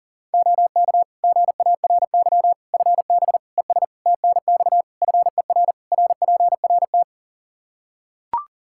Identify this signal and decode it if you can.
Morse code, heard as a keyed tone, about 40 wpm, 700 Hz. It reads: OK GARY FB ES TNX FER RPRT
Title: Practice QSO - Exchange 2 - Info - 40wpm